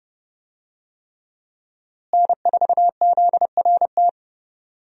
Morse code, heard as a keyed tone, about 30 wpm, 700 Hz. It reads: N4ZRT
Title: USA Call Signs - 5 Characters - Repeated 3x - 30wpm